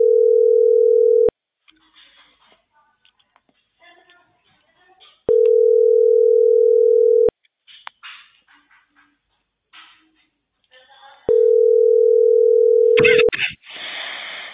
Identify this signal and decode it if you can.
Ringback tone (North America), 440 + 480 Hz.
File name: outgoing recordings